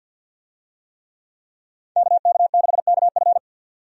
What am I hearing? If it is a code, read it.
Morse code, heard as a keyed tone, about 48 wpm, 700 Hz. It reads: KK6KF
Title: USA Call Signs - 5 Characters - 50wpm